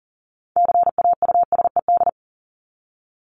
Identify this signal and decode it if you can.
Morse code, heard as a keyed tone, about 40 wpm, 700 Hz. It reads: CAUSED
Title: ED Suffix - Top 100 Words with Base Words 40wpm